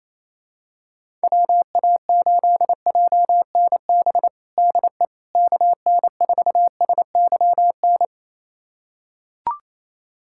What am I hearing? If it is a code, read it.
Morse code, heard as a keyed tone, about 28 wpm, 700 Hz. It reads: WA8JNB DE KN4SYN
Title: USA Callsign DE Callsign - 6 Characters - 28wpm